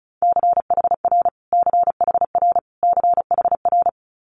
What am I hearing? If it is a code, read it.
Morse code, heard as a keyed tone, about 36 wpm, 700 Hz. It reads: CHR CHR CHR
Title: Multi-letter Phonemes Repeated 3x 35wpm